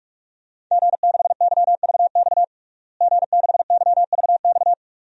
Morse code, heard as a keyed tone, about 46 wpm, 700 Hz. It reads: G6YVX G6YVX
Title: DXCC-100 Call Signs - 5 Characters - Repeated 3x - 45wpm